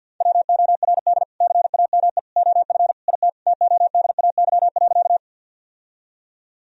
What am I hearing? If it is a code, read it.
Morse code, heard as a keyed tone, about 50 wpm, 700 Hz. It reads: WORD CAME OF IT TODAY.
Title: Sentences from Top 1000 Words - Review Part 4 - 50wpm